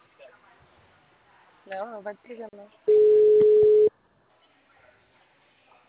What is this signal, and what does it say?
Ringback tone (Europe, 425 Hz), with speech.